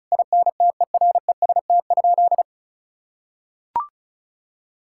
Morse code, heard as a keyed tone, about 36 wpm, 700 Hz. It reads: INTEREST?
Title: Sentences from Top 1000 Words - Review Part 5 - 35wpm